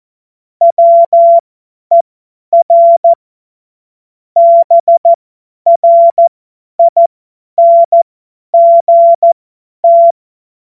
Morse code, heard as a keyed tone, about 14 wpm, spaced as for 10 wpm, 680 Hz. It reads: WER BRINGT